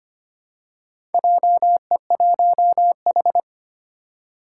Morse code, heard as a keyed tone, about 25 wpm, 700 Hz. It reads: JE1H
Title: Four Challenging Letter-Number Combinations 25wpm